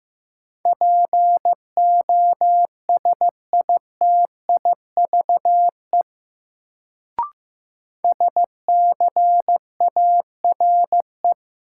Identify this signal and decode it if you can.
Morse code, heard as a keyed tone, about 15 wpm, 700 Hz. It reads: POSITIVE SCARE
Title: Sentences from Top 1000 Words - Review Part 5 - New Word Drill - 15wpm